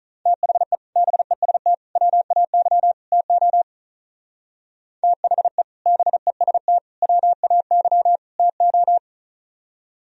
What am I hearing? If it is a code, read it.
Morse code, heard as a keyed tone, about 44 wpm, 700 Hz. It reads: THE BEST WAY TO THE BEST WAY TO